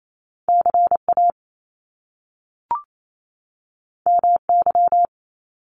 Morse code, heard as a keyed tone, about 28 wpm, 700 Hz. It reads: CA MY
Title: Top Two Letter Combinations in Words 28wpm